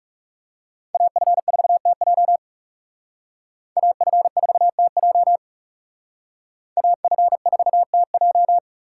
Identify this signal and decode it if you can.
Morse code, heard as a keyed tone, about 36 wpm, 700 Hz. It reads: AF4TJ AF4TJ AF4TJ
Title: Speed Racing - USA 2x2 Call Signs 30wpm